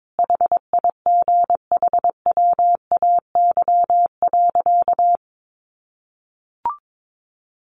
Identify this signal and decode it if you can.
Morse code, heard as a keyed tone, about 22 wpm, 700 Hz. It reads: HIGHWAY.
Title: Sentences from Top 1400 Words 22wpm